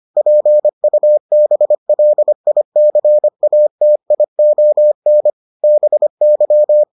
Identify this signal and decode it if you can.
Morse code, heard as a keyed tone, about 25 wpm, 600 Hz. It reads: PUBLICATION BY